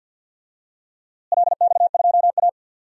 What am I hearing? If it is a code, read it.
Morse code, heard as a keyed tone, about 50 wpm, 700 Hz. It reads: LX2A
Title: DXCC-100 Call Signs - 4 Characters - Repeated 2x - 50wpm